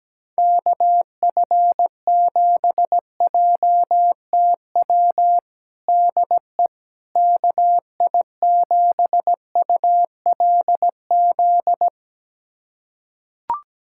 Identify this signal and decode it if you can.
Morse code, heard as a keyed tone, about 17 wpm, 700 Hz. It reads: KF7JTW DE KI7ULZ
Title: USA Callsign DE Callsign 17wpm